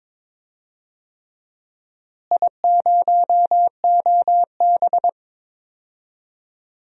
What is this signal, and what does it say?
Morse code, heard as a keyed tone, about 22 wpm, 700 Hz. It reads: I0OB
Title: DXCC-100 Call Signs - 4 Characters - Repeated 3x - 22wpm